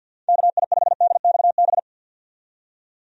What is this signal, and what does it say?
Morse code, heard as a keyed tone, about 51 wpm, 700 Hz. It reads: KI5DXB
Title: USA Call Signs - 6 Characters - 50wpm